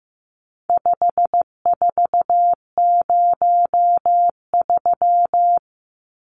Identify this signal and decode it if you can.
Morse code, heard as a keyed tone, about 15 wpm, 700 Hz. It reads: 5403
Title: Numbers - 4-Digit 15wpm